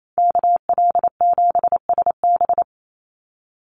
Morse code, heard as a keyed tone, about 28 wpm, 700 Hz. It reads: KL7SB
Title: Call Signs - 28wpm